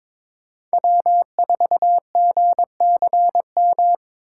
Morse code, heard as a keyed tone, about 22 wpm, 700 Hz. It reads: W4GCM
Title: USA Call Signs - 5 Characters - 2x Character Spacing - 22wpm